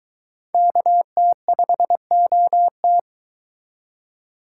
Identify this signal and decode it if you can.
Morse code, heard as a keyed tone, about 22 wpm, 700 Hz. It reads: KT5OT